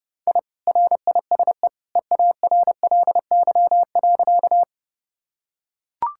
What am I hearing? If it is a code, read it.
Morse code, heard as a keyed tone, about 32 wpm, 700 Hz. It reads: I RISE EARLY.
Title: Sentences from Top 1000 Words - Review Part 6 - 30wpm